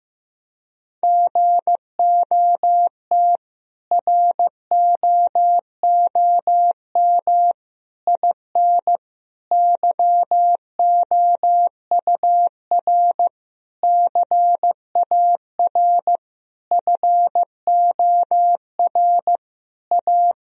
Morse code, heard as a keyed tone, about 15 wpm, 700 Hz. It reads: GOT ROOM IN YOUR CAR FOR A
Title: Sentences from Top 1000 Words - Review Part 1 - 15wpm